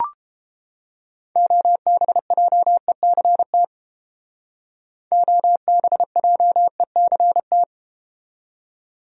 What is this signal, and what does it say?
Morse code, heard as a keyed tone, about 33 wpm, 700 Hz. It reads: OBJECT OBJECT